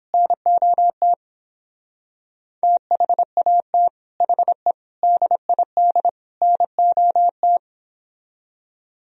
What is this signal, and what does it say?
Morse code, heard as a keyed tone, about 29 wpm, 700 Hz. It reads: NOT THAT HE DID NOT